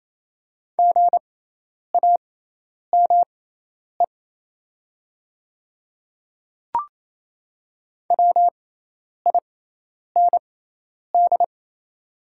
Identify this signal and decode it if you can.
Morse code, heard as a keyed tone, about 28 wpm, spaced as for 10 wpm, 700 Hz. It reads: GAME WIND
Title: 6x Character Spacing - Top 500 Words - 28wpm